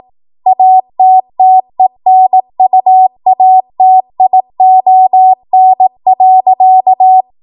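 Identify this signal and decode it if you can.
Morse code, heard as a keyed tone, about 18 wpm, 750 Hz. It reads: ATTENUATION.
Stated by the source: ARRL 18 wpm Morse code practice archive transmission.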